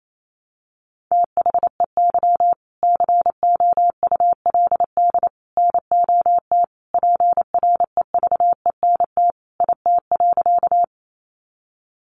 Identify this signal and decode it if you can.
Morse code, heard as a keyed tone, about 28 wpm, 700 Hz. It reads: THEY COULD NOT PREVENT IT.
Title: Sentences from Top 1100 Words 28wpm